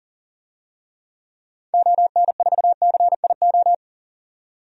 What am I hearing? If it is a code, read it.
Morse code, heard as a keyed tone, about 40 wpm, 700 Hz. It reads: ON4CIO
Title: DXCC-100 Call Signs - 6 Characters - 40wpm